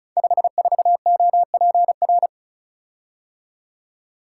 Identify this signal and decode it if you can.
Morse code, heard as a keyed tone, about 36 wpm, 700 Hz. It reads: 54OPR
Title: Code Groups - 5 Letter:Numbers - 35wpm